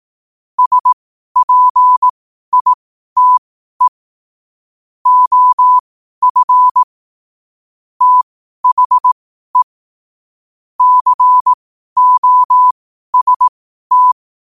Morse code, heard as a keyed tone, about 18 wpm, spaced as for 12 wpm, 1000 Hz. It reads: SPITE OF THE COST
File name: qotdfiles / 12wpm